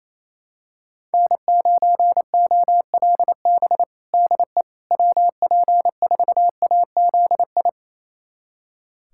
Morse code, heard as a keyed tone, about 27 wpm, 700 Hz. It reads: N9OLB DE WP4AZI
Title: USA Callsign DE Callsign 28wpm